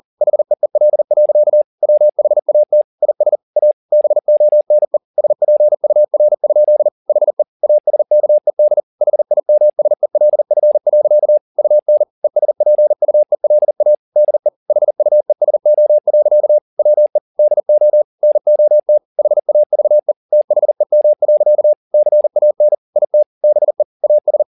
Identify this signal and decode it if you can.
Morse code, heard as a keyed tone, about 40 wpm, 600 Hz. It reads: HEEL. WHAT IS A BONE SPUR? HE ASKED HIMSELF. UN ESPUELA DE HUESO. WE DO NOT HAVE THEM. CAN IT BE AS